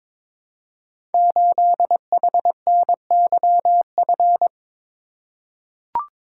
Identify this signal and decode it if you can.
Morse code, heard as a keyed tone, about 22 wpm, 700 Hz. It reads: 8HNYF
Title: Code Groups - 5 Letter:Numbers - 22wpm